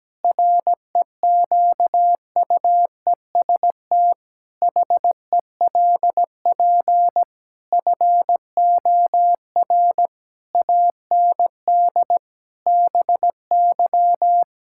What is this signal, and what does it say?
Morse code, heard as a keyed tone, about 17 wpm, 700 Hz. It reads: REQUEST HELP FOR AND BY
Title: Pangram Challenge v2 - 17wpm